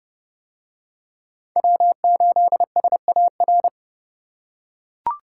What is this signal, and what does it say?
Morse code, heard as a keyed tone, about 30 wpm, 700 Hz. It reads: W8SAR